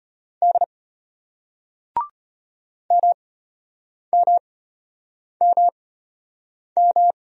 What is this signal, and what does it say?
Morse code, heard as a keyed tone, about 34 wpm, spaced as for 8 wpm, 700 Hz. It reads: D MMMM